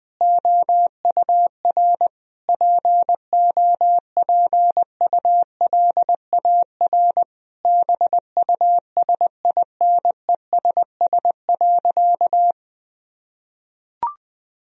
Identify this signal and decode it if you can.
Morse code, heard as a keyed tone, about 21 wpm, 700 Hz. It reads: OUR POPULAR BUSINESS.
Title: Sentences from Top 1000 Words - Review Part 3 - 20wpm